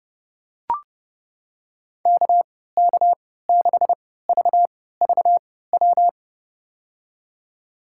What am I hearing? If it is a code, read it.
Morse code, heard as a keyed tone, about 32 wpm, spaced as for 17 wpm, 700 Hz. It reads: KK6VVW
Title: USA Call Signs - 6 Characters - 3x Character Spacing - 30wpm